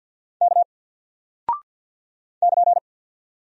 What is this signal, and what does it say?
Morse code, heard as a keyed tone, about 50 wpm, 700 Hz. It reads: K <KN>
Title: Single Character (Letter:Number:Punctuation:Prosign) Alt 50wpm